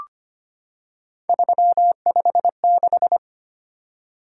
Morse code, heard as a keyed tone, about 25 wpm, 700 Hz. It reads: 356
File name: Three Letter-Number Combinations - 25wpm